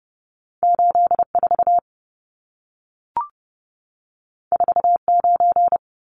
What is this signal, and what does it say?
Morse code, heard as a keyed tone, about 30 wpm, 700 Hz. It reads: 84 49
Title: Numbers - 2-Digit 30wpm